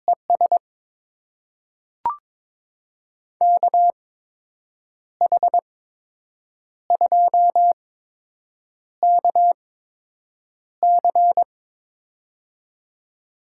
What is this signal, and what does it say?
Morse code, heard as a keyed tone, about 22 wpm, spaced as for 7 wpm, 700 Hz. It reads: ES KH2KC
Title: USA Call Signs - 5 Characters - 8x Character Spacing - 22wpm